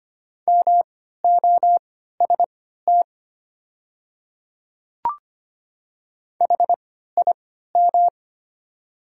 Morse code, heard as a keyed tone, about 25 wpm, spaced as for 14 wpm, 690 Hz. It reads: MOST HIM
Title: 3x Character Spacing - Top 100 Words - 25wpm